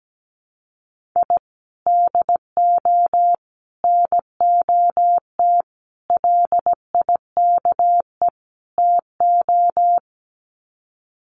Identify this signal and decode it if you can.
Morse code, heard as a keyed tone, about 17 wpm, 700 Hz. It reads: I DO NOT LIKE TO
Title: Sets of 5 Words 17wpm